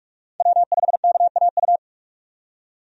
Morse code, heard as a keyed tone, about 45 wpm, 700 Hz. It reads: W5KAU
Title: USA Call Signs - 5 Characters - 45wpm